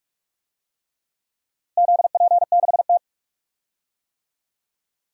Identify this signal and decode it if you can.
Morse code, heard as a keyed tone, about 45 wpm, 700 Hz. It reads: ZP6T
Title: DXCC-100 Call Signs - 4 Characters - Repeated 3x - 45wpm